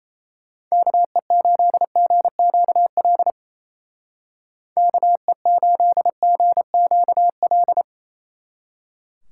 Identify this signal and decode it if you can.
Morse code, heard as a keyed tone, about 34 wpm, 700 Hz. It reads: KE8GQL KE8GQL
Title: Speed Racing - USA Call Signs 28wpm